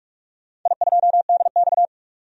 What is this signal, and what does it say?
Morse code, heard as a keyed tone, about 45 wpm, 700 Hz. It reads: I2DX
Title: DXCC-100 Call Signs - 4 Characters - 45wpm